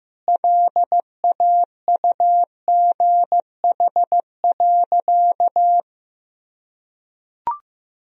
Morse code, heard as a keyed tone, about 15 wpm, 700 Hz. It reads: LAUGH.
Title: Sentences from Top 1000 Words - Review Part 2 - 15wpm